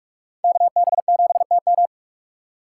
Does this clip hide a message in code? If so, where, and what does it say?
Yes: Morse code from 0.4 to 1.9 s, heard as a keyed tone, about 42 wpm, 700 Hz. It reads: KB7TM